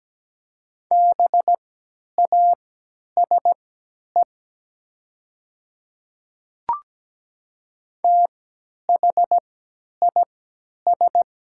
Morse code, heard as a keyed tone, about 17 wpm, spaced as for 10 wpm, 700 Hz. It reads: BASE THIS
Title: 3x Character Spacing - Top 300 Words - 17wpm